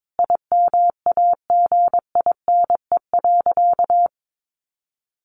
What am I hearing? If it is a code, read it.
Morse code, heard as a keyed tone, about 22 wpm, 700 Hz. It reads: IMAGINE.
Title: Sentences from Top 1400 Words 22wpm